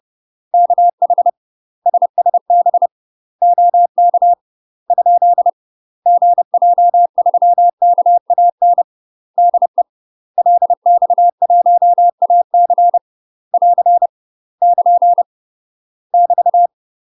Morse code, heard as a keyed tone, about 30 wpm, 700 Hz. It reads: KH SSB OK ? GJ3KAN DE LX1AC <AR> <KN> <BT>
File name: UK MASTER QSOs TRAINING/30WPM